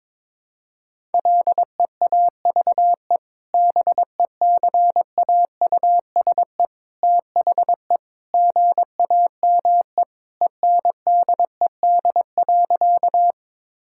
Morse code, heard as a keyed tone, about 22 wpm, 700 Hz. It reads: LEAVE BECAUSE THE GAME ENDED.
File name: practice / 22wpm / Sentences with 2-Word Phrases - 22wpm